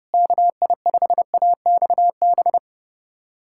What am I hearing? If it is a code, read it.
Morse code, heard as a keyed tone, about 32 wpm, 700 Hz. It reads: KI5AXB
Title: USA Callsign DE Callsign 30wpm